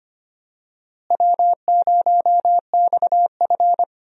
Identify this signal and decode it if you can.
Morse code, heard as a keyed tone, about 25 wpm, 700 Hz. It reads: W0XF
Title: USA Call Signs - 4 Characters - Repeated 3x - 25wpm